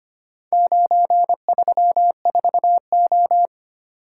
Morse code, heard as a keyed tone, about 25 wpm, 700 Hz. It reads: 934O
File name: practice / 25wpm / Four Letter-Number Combinations - 25wpm